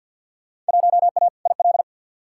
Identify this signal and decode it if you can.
Morse code, heard as a keyed tone, about 51 wpm, 700 Hz. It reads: 1A IL
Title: Field Day Exchanges Encore - 50wpm